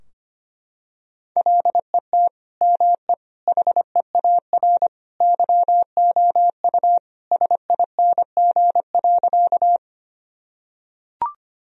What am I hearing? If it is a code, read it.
Morse code, heard as a keyed tone, about 25 wpm, 700 Hz. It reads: LET ME HEAR YOU SING.
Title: Sentences from Top 1000 Words - Review Part 2 - 25wpm